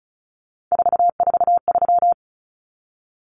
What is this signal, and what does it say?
Morse code, heard as a keyed tone, about 34 wpm, 700 Hz. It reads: 443
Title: Common Cut Numbers - 3-Digit 35wpm